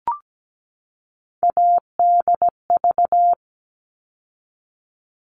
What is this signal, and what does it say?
Morse code, heard as a keyed tone, about 17 wpm, 700 Hz. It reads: ADV
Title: All Cut Numbers - 3-Digit 17wpm